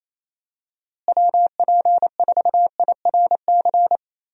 Morse code, heard as a keyed tone, about 28 wpm, 700 Hz. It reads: WP4IRC
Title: USA Call Signs - 6 Characters - Repeated 3x - 28wpm